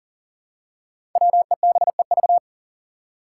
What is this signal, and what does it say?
Morse code, heard as a keyed tone, about 41 wpm, 700 Hz. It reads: WEBEV
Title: Five Challenging Letter-Number Combinations 40wpm